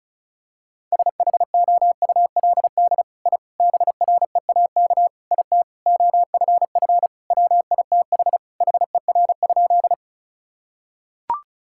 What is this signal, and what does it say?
Morse code, heard as a keyed tone, about 36 wpm, 700 Hz. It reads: SHOULD I BREAK IT OFF WITH HER?
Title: Sentences from Top 1000 Words - Review Part 6 - 35wpm